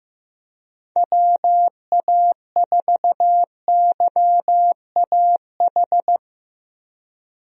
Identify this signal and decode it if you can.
Morse code, heard as a keyed tone, about 15 wpm, 700 Hz. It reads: WA4YAH
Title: USA Call Signs - 6 Characters - Repeated 2x - 15wpm